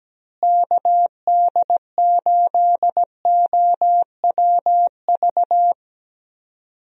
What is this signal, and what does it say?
Morse code, heard as a keyed tone, about 17 wpm, 700 Hz. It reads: KD8OWV
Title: USA Call Signs - 6 Characters - Repeated 2x - 17wpm